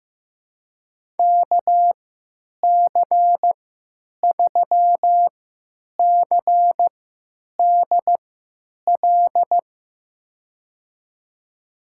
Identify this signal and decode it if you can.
Morse code, heard as a keyed tone, about 15 wpm, spaced as for 9 wpm, 700 Hz. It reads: KC3CDL
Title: USA Call Signs - 3x Character Spacing - 15wpm